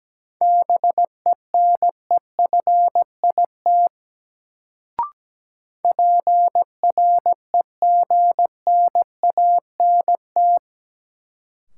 Morse code, heard as a keyed tone, about 17 wpm, 700 Hz. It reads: BENEFIT PREGNANT
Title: Sentences from Top 1000 Words - Review Part 6 - New Word Drill - 17wpm